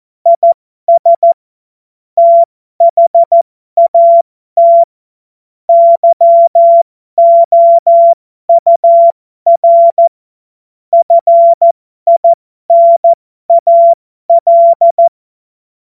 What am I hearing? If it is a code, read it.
Morse code, heard as a keyed tone, about 14 wpm, spaced as for 12 wpm, 680 Hz. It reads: IS THAT YOUR FINAL
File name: Morsekurs / Witze